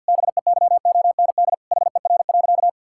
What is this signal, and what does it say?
Morse code, heard as a keyed tone, about 52 wpm, 700 Hz. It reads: BEYOND HER.
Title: Sentences from Top 1000 Words - Review Part 6 - 50wpm